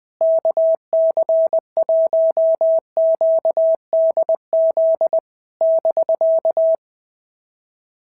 Morse code, heard as a keyed tone, about 21 wpm, 660 Hz. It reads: KC1QDZ <BK>
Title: POTA QSO Practice - 20wpm